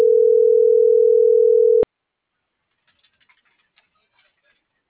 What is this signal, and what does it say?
Ringback tone (North America), 440 + 480 Hz.